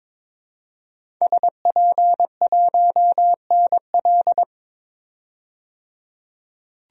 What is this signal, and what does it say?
Morse code, heard as a keyed tone, about 21 wpm, 700 Hz. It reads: SP1NL
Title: DXCC-100 Call Signs - 5 Characters - 22wpm